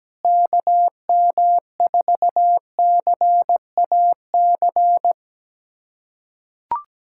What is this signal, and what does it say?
Morse code, heard as a keyed tone, about 17 wpm, 700 Hz. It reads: KM4CAC
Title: USA Callsign DE Callsign 17wpm